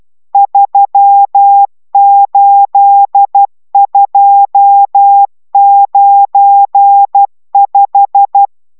Morse code, heard as a keyed tone, about 12 wpm, 800 Hz. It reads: 38295